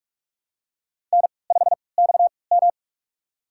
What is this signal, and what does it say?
Morse code, heard as a keyed tone, about 44 wpm, spaced as for 25 wpm, 700 Hz. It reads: N5XM
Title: USA Call Signs - 4 Characters - 3x Character Spacing - 45wpm